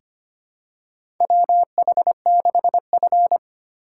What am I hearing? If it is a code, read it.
Morse code, heard as a keyed tone, about 25 wpm, 700 Hz. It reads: WH6F
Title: USA Call Signs - 4 Characters - 4x Character Spacing - 25wpm